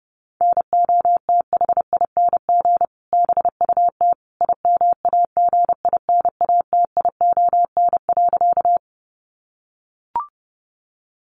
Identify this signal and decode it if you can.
Morse code, heard as a keyed tone, about 30 wpm, 700 Hz. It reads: NOTHING BUT IMAGINATION.
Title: Sentences from Top 1300 Words 30wpm